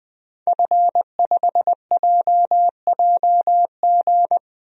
Morse code, heard as a keyed tone, about 20 wpm, 700 Hz. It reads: F5JJG
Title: DXCC-100 Call Signs - 5 Characters - Repeated 2x - 20wpm